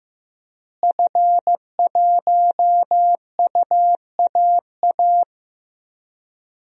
Morse code, heard as a keyed tone, about 15 wpm, 700 Hz. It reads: F1UAA
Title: DXCC-100 Call Signs - 5 Characters - Repeated 3x - 15wpm